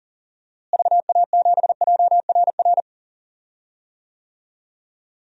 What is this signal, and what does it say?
Morse code, heard as a keyed tone, about 40 wpm, 700 Hz. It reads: VA7JRR
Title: DXCC-100 Call Signs - 6 Characters - Repeated 3x - 40wpm